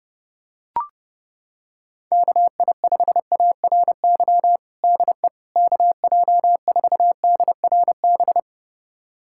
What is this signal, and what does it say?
Morse code, heard as a keyed tone, about 30 wpm, 700 Hz. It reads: KI5ARY DE KJ4DRB
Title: USA Callsign DE Callsign 30wpm